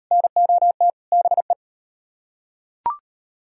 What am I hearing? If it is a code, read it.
Morse code, heard as a keyed tone, about 38 wpm, 700 Hz. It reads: NOT BE